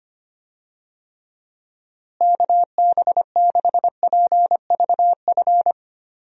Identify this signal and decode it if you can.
Morse code, heard as a keyed tone, about 25 wpm, 700 Hz. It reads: KB6PVF
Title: USA Call Signs - 6 Characters - Repeated 2x - 25wpm